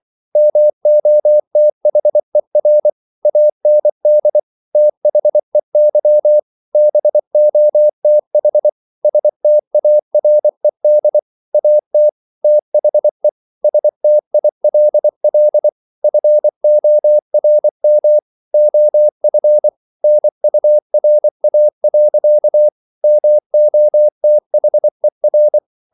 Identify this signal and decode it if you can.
Morse code, heard as a keyed tone, about 24 wpm, 600 Hz. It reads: MOTHER AND THEY BOTH STARED AT THE STILL FORM OF NURA. MOTHER